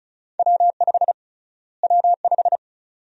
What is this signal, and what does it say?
Morse code, heard as a keyed tone, about 36 wpm, 700 Hz. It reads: W5 W5
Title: USA Call Sign Prefixes - Repeated 2x - 35wpm